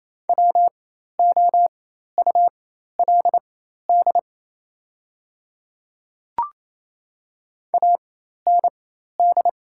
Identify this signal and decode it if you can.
Morse code, heard as a keyed tone, about 28 wpm, spaced as for 13 wpm, 700 Hz. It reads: WOULD AND
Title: 4x Character Spacing - Top 300 Words - 28wpm